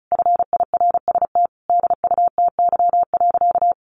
Morse code, heard as a keyed tone, about 34 wpm, 700 Hz. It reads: FIRST DUTY.
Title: Sentences from Top 1500 Words 35wpm